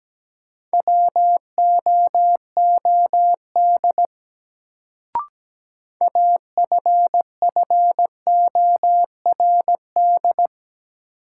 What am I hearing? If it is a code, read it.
Morse code, heard as a keyed tone, about 17 wpm, 700 Hz. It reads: WOOD AFFORD
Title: Sentences from Top 1100 Words - New Word Drill Alt - 17wpm